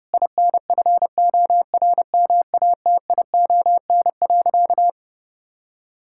Morse code, heard as a keyed tone, about 30 wpm, 700 Hz. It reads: INFORMATION.
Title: Sentences from Top 1000 Words - Review Part 6 - 30wpm